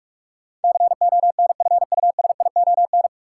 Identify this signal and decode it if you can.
Morse code, heard as a keyed tone, about 42 wpm, 700 Hz. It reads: CONFUSION
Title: Sentences from Top 1600 Words - New Word Drill - 45wpm